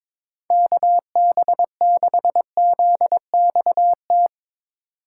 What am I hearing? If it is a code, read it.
Morse code, heard as a keyed tone, about 22 wpm, 700 Hz. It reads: KB6ZXT
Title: Speed Racing - USA 2x3 Call Signs 22wpm